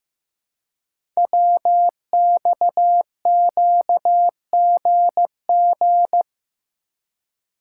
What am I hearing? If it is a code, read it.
Morse code, heard as a keyed tone, about 15 wpm, 700 Hz. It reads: WXQGG
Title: Code Groups - 5 Letters - 15wpm